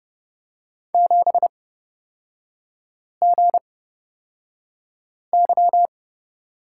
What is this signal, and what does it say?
Morse code, heard as a keyed tone, about 30 wpm, spaced as for 5 wpm, 700 Hz. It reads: 7GY
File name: Lesson 37e - Letter x - Three Characters 30wpm